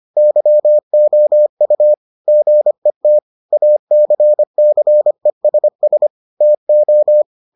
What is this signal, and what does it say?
Morse code, heard as a keyed tone, about 25 wpm, 600 Hz. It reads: YOU GET ACCESS TO